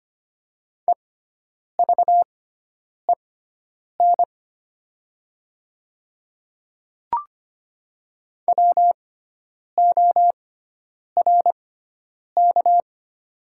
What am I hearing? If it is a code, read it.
Morse code, heard as a keyed tone, about 25 wpm, spaced as for 9 wpm, 700 Hz. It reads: EVEN WORK
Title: 6x Character Spacing - Top 300 Words - 25wpm